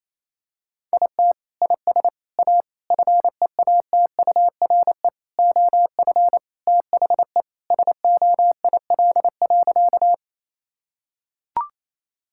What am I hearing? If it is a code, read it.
Morse code, heard as a keyed tone, about 28 wpm, 700 Hz. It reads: IT IS A FEATURE OF THE SOIL.
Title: Sentences from Top 1000 Words - Review Part 5 - 28wpm